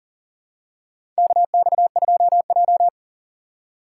Morse code, heard as a keyed tone, about 42 wpm, 700 Hz. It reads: KX2J
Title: USA Call Signs - 4 Characters - 4x Character Spacing - 40wpm